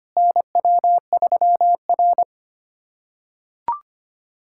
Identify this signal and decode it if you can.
Morse code, heard as a keyed tone, about 25 wpm, 700 Hz. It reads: NW3R
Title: USA Callsign DE Callsign - 4 Characters - 25wpm